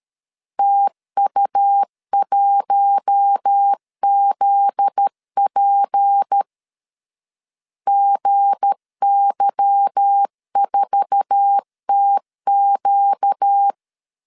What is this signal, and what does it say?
Morse code, heard as a keyed tone, about 13 wpm, 800 Hz. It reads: TU1ZP GY4TQ
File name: LEZIONI / 8 YJ45 / 8 YJ4 12wpm 2151